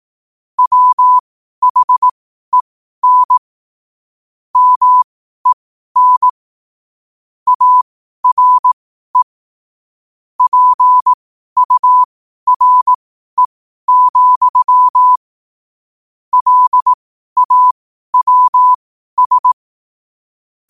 Morse code, heard as a keyed tone, about 18 wpm, spaced as for 12 wpm, 1000 Hz. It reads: WHEN MEN ARE PURE, LAWS
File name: qotdfiles / 12wpm